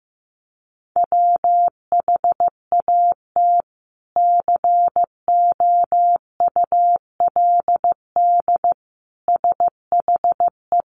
Morse code, heard as a keyed tone, about 15 wpm, 700 Hz. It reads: WHAT COULD SHE
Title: Sentences from Top 200 Words 15wpm